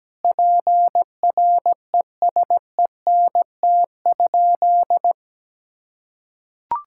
Morse code, heard as a keyed tone, about 17 wpm, 700 Hz. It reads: PRESENT?
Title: Sentences from Top 1000 Words - Review Part 6 - 17wpm